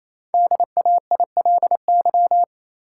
Morse code, heard as a keyed tone, about 28 wpm, 700 Hz. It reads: DAILY
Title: Sentences from Top 1100 Words - New Word Drill - 28wpm